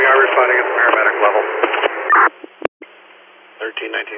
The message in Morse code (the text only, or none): none